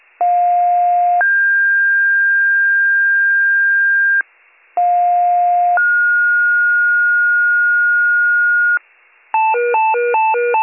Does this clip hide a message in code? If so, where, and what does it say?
Yes: two-tone page from 0.2 to 8.8 s, heard as a one-second tone, then a three-second tone, about 710 then 1600 Hz, about 710 then 1430 Hz, with noise.